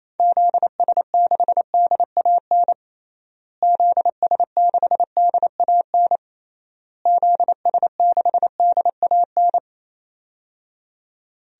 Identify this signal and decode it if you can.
Morse code, heard as a keyed tone, about 28 wpm, 700 Hz. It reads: ZS6DAN ZS6DAN ZS6DAN
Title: DXCC-100 Call Signs - 6 Characters - Repeated 3x - 28wpm